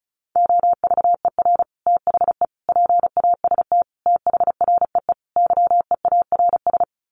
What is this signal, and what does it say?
Morse code, heard as a keyed tone, about 34 wpm, 700 Hz. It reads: OVER THE PAST THREE YEARS
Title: Sets of 5 Words Encore2 35wpm